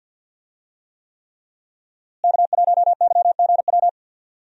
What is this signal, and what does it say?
Morse code, heard as a keyed tone, about 50 wpm, 700 Hz. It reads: K1YGW
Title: USA Call Signs - 5 Characters - Repeated 3x - 50wpm